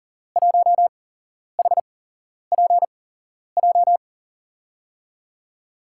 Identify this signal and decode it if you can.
Morse code, heard as a keyed tone, about 40 wpm, spaced as for 11 wpm, 700 Hz. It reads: 1HPJ
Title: USA Call Signs - 6 Characters - 8x Character Spacing - 40wpm